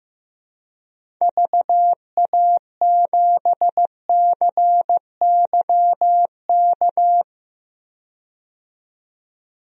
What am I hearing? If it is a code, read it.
Morse code, heard as a keyed tone, about 15 wpm, 700 Hz. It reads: VA7CYK
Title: DXCC-100 Call Signs - 6 Characters - 15wpm